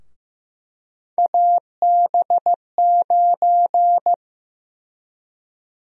Morse code, heard as a keyed tone, about 15 wpm, 700 Hz. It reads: AB9
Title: USA Call Sign Prefixes - 15wpm